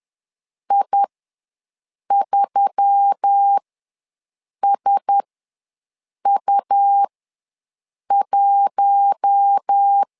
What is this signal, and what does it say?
Morse code, heard as a keyed tone, about 10 wpm, spaced as for 6 wpm, 800 Hz. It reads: I3SU1
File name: LEZIONI / 4 Q1S3 ws 10wpm 2437